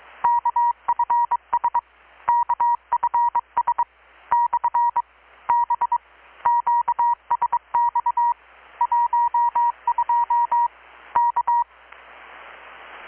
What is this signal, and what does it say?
Morse code, heard as a keyed tone, about 22 wpm, 950 Hz. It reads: KFS KFS / B QSX 12 K
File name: blog / swl / 6-abril-2019-12695-USB-KPS